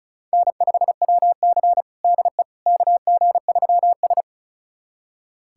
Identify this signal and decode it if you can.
Morse code, heard as a keyed tone, about 36 wpm, 700 Hz. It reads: N5WC DE KG3S
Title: USA Callsign DE Callsign - 4 Characters - 35wpm